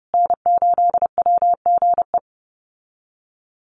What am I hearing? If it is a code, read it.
Morse code, heard as a keyed tone, about 30 wpm, 700 Hz. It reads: N8WGE
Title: USA Call Signs Repeated 3x 30wpm